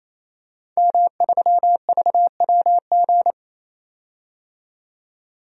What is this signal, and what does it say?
Morse code, heard as a keyed tone, about 28 wpm, 700 Hz. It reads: M3VWG
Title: DXCC-100 Call Signs - 5 Characters - 28wpm